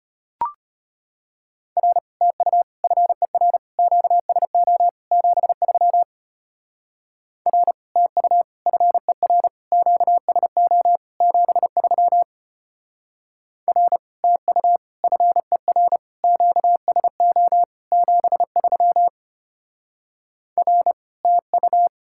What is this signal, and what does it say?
Morse code, heard as a keyed tone, about 31 wpm, 700 Hz. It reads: R TU FER QSO 73 R TU FER QSO 73 R TU FER QSO 73 R TU
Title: Speed-Racing Common QSO Phrases - 25wpm